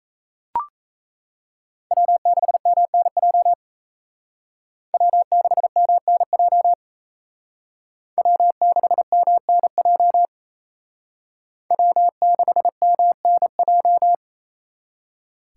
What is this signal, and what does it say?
Morse code, heard as a keyed tone, about 34 wpm, 700 Hz. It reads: W6MNJ W6MNJ W6MNJ W6MNJ
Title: Speed Racing - USA 1x3 Call Signs 28wpm